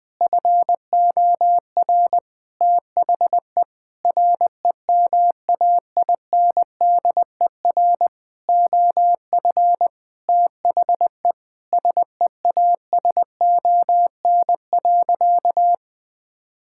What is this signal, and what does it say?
Morse code, heard as a keyed tone, about 20 wpm, 700 Hz. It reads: FOR THE REMAINDER OF THE SEASON.
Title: Sentences with 6-Word Phrases - 20wpm